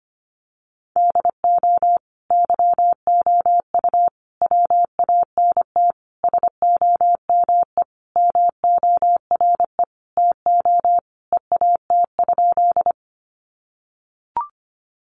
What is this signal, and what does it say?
Morse code, heard as a keyed tone, about 25 wpm, 700 Hz. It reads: DO YOU WANT SOME MORE TO EAT?
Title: Sentences from Top 300 Words 25wpm